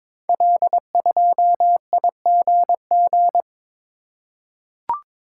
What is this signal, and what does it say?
Morse code, heard as a keyed tone, about 22 wpm, 700 Hz. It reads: L2IGG
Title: Code Groups - 5 Letter:Numbers - 22wpm